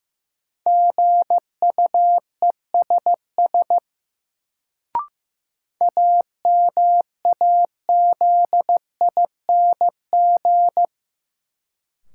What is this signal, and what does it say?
Morse code, heard as a keyed tone, about 15 wpm, 700 Hz. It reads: GUESS AMAZING
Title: Sentences from Top 1500 Words - New Word Drill Alt - 15wpm